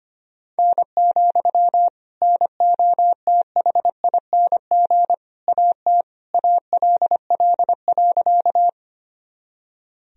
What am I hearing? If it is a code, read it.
Morse code, heard as a keyed tone, about 25 wpm, 700 Hz. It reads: N, NOTHING AT ALL.
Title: Sentences from Top 500 Words and Abbreviations 25wpm